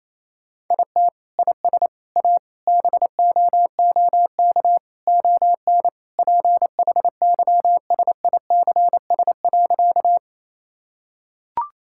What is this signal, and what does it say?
Morse code, heard as a keyed tone, about 28 wpm, 700 Hz. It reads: IT IS A BOOK ON PHYSICS.
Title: Sentences from Top 1000 Words - Review Part 5 - 28wpm